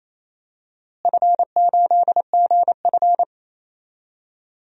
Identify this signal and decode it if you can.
Morse code, heard as a keyed tone, about 28 wpm, 700 Hz. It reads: F8GF
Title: DXCC-100 Call Signs - 4 Characters - Repeated 2x - 28wpm